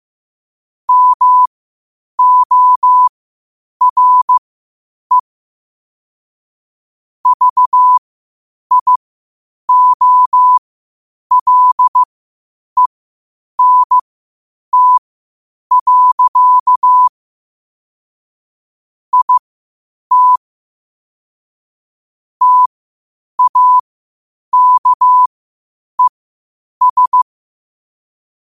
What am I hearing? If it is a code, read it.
Morse code, heard as a keyed tone, about 15 wpm, spaced as for 8 wpm, 1000 Hz. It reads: MORE VIOLENT. IT TAKES
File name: qotdfiles / 8wpm